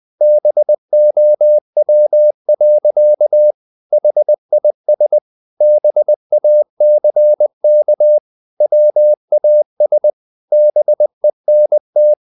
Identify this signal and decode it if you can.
Morse code, heard as a keyed tone, about 20 wpm, 600 Hz. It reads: BOW. HIS BACK WAS BENT
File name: the old man and the sea part 2 20wpm